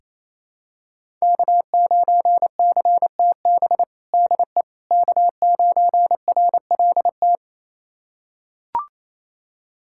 Morse code, heard as a keyed tone, about 28 wpm, 700 Hz. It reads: K9CTB DE K9RLT
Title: USA Callsign DE Callsign - 5 Characters - 28wpm